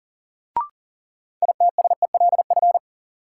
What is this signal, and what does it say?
Morse code, heard as a keyed tone, about 40 wpm, 700 Hz. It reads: ITSELF
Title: Sentences from Top 800 Words - New Word Drill - 40wpm